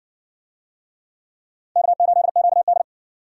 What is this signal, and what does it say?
Morse code, heard as a keyed tone, about 60 wpm, 700 Hz. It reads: K8QD